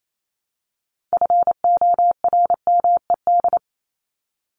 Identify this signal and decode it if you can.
Morse code, heard as a keyed tone, about 28 wpm, 700 Hz. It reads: FORMED
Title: ED Suffix - Top 100 Words with Base Words 28wpm